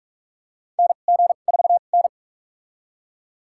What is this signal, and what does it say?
Morse code, heard as a keyed tone, about 41 wpm, spaced as for 31 wpm, 700 Hz. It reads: NG4N
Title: USA Call Signs - 4 Characters - 2x Character Spacing - 45wpm